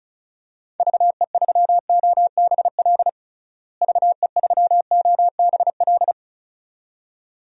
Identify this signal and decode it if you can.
Morse code, heard as a keyed tone, about 36 wpm, 700 Hz. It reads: VE3OBL VE3OBL
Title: DXCC-100 Call Signs - 6 Characters - Repeated 2x - 35wpm